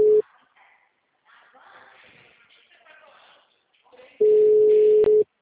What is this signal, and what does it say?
Ringback tone (Europe, 425 Hz), with speech.
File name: out- -2050- - - .40365